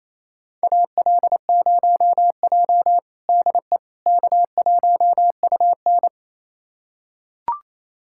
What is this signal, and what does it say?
Morse code, heard as a keyed tone, about 28 wpm, 700 Hz. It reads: AL0J DE K1UN
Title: USA Callsign DE Callsign - 4 Characters - 28wpm